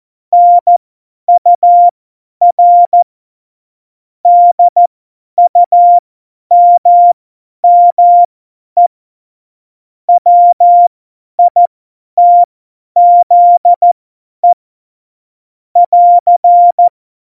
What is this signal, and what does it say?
Morse code, heard as a keyed tone, about 14 wpm, spaced as for 10 wpm, 700 Hz. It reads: NUR DUMME WITZE <AR>
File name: Morsekurs / Witze / Witz 13. Februar 2018, 50 F70